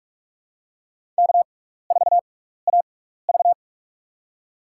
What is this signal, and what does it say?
Morse code, heard as a keyed tone, about 44 wpm, spaced as for 15 wpm, 700 Hz. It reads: K4AV
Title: USA Call Signs - 4 Characters - 6x Character Spacing - 45wpm